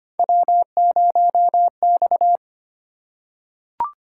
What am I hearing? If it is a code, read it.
Morse code, heard as a keyed tone, about 25 wpm, 700 Hz. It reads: W0X
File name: Three Letter-Number Combinations - 25wpm